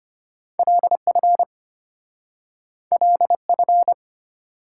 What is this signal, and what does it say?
Morse code, heard as a keyed tone, about 34 wpm, 700 Hz. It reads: LF LF